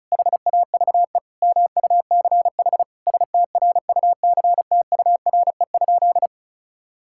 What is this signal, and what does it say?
Morse code, heard as a keyed tone, about 36 wpm, 700 Hz. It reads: HAVE MUCH STRUCTURE?
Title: Sentences from Top 1000 Words - Review Part 6 - 35wpm